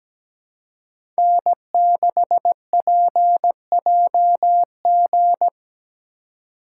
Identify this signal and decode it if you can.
Morse code, heard as a keyed tone, about 17 wpm, 700 Hz. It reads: N6PJG
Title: USA Call Signs - 5 Characters - 3x Character Spacing - 17wpm